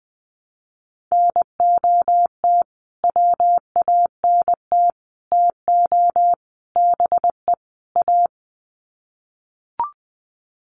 Morse code, heard as a keyed tone, about 20 wpm, 700 Hz. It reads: NOT WANT TO BE A
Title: Sets of 5 Words Encore 20wpm